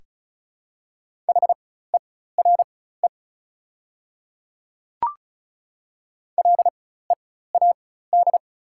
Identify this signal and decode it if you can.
Morse code, heard as a keyed tone, about 36 wpm, spaced as for 16 wpm, 690 Hz. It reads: HERE LEAD
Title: 4x Character Spacing - Top 500 Words - 35wpm